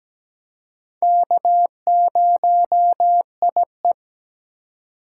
Morse code, heard as a keyed tone, about 17 wpm, 700 Hz. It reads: K0IE
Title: USA Call Signs - 4 Characters - 2x Character Spacing - 17wpm